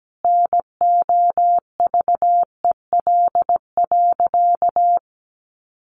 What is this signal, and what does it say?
Morse code, heard as a keyed tone, about 17 wpm, 700 Hz. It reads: NOVEL.
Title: Sentences from Top 1300 Words 17wpm